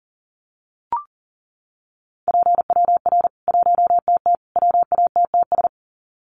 Morse code, heard as a keyed tone, about 40 wpm, 700 Hz. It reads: PWR 1TT WATTS
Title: Common QSO Phrases 40wpm